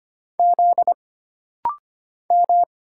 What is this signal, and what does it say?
Morse code, heard as a keyed tone, about 25 wpm, 700 Hz. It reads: Z M
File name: Single Character (Letter:Number:Punctuation:Prosign) - Rapid-Fire - 25wpm